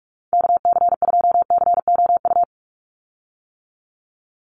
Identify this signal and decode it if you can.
Morse code, heard as a keyed tone, about 44 wpm, 700 Hz. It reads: KC2COU
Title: USA Call Signs 45wpm